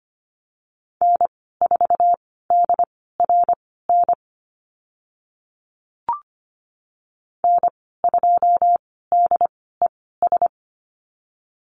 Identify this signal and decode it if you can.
Morse code, heard as a keyed tone, about 25 wpm, spaced as for 16 wpm, 700 Hz. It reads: N4DRN N2DES
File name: Lesson 15g - Number 5 - Call Signs 25wpm